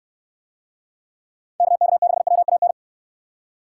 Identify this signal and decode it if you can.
Morse code, heard as a keyed tone, about 63 wpm, 700 Hz. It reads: KK6WAN